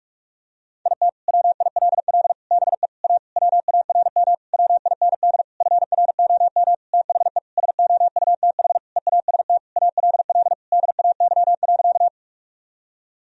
Morse code, heard as a keyed tone, about 41 wpm, 700 Hz. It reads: IT WILL BE A WARM WIND FROM THE SOUTH EAST ALL DAY.
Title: Sentences from Top 1000 Words - Review Part 3 - 45wpm